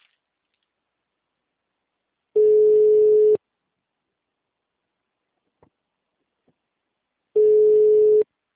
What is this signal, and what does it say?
Ringback tone (Europe, 425 Hz).
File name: out- -2006